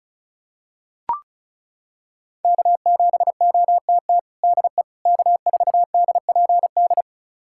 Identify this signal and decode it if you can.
Morse code, heard as a keyed tone, about 36 wpm, 700 Hz. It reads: K7OTT DE K4DPD
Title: USA Callsign DE Callsign - 5 Characters - 35wpm